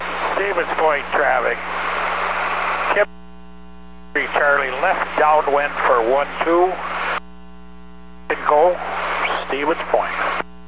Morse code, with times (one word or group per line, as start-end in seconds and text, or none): none